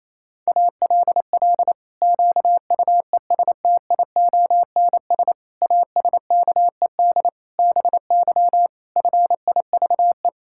Morse code, heard as a keyed tone, about 27 wpm, 700 Hz. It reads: ALL QUESTIONS ASKED BY FIVE
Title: Pangram Challenge v2 - 28wpm